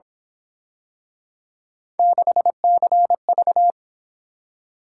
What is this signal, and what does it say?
Morse code, heard as a keyed tone, about 26 wpm, 700 Hz. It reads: 6CV
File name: Speed-Racing - Three Letter-Number Combinations - 22wpm